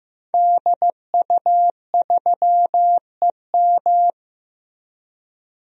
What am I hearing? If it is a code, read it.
Morse code, heard as a keyed tone, about 15 wpm, 700 Hz. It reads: DU3EM
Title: DXCC-100 Call Signs - 5 Characters - Repeated 3x - 15wpm